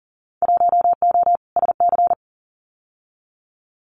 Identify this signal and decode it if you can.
Morse code, heard as a keyed tone, about 38 wpm, 700 Hz. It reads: 1O SC
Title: Winter Field Day Exchanges 40wpm